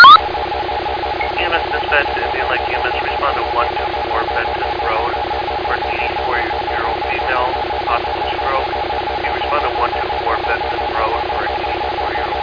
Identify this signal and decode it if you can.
Two-tone teleprinter signal, about 450 Hz shift.
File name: Indian Lake EMS